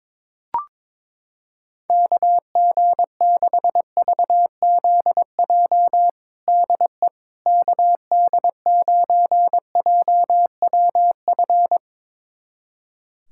Morse code, heard as a keyed tone, about 22 wpm, 700 Hz. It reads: KG6VZJ DE KD9JWF
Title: USA Callsign DE Callsign - 6 Characters - 22wpm